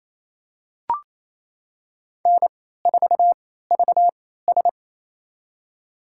Morse code, heard as a keyed tone, about 28 wpm, spaced as for 16 wpm, 700 Hz. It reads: N4VS
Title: USA Call Signs - 4 Characters - 3x Character Spacing - 28wpm